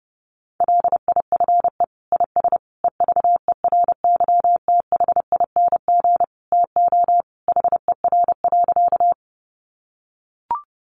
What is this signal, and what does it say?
Morse code, heard as a keyed tone, about 30 wpm, 700 Hz. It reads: LIFE IS EVERYTHING TO HER.
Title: Sentences from Top 700 Words 30wpm